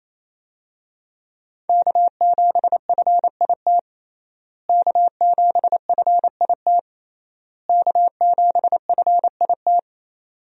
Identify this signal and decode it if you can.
Morse code, heard as a keyed tone, about 28 wpm, 700 Hz. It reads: K7FIT K7FIT K7FIT
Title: USA Call Signs - 5 Characters - Repeated 3x - 28wpm